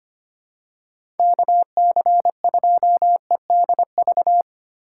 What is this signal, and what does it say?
Morse code, heard as a keyed tone, about 25 wpm, 700 Hz. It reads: KC2EDV
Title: USA Call Signs - 6 Characters - 3x Character Spacing - 25wpm